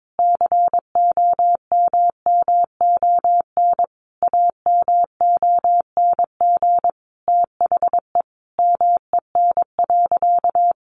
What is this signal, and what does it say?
Morse code, heard as a keyed tone, about 22 wpm, 700 Hz. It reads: COMMON AMONG THE MEN.
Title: Sentences from Top 1400 Words 22wpm